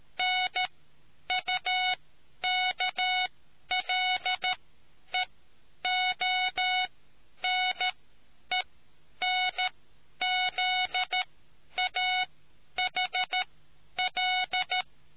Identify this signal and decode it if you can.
Morse code, heard as a keyed tone, about 13 wpm, spaced as for 9 wpm, 740 Hz. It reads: NUKLEONENZAHL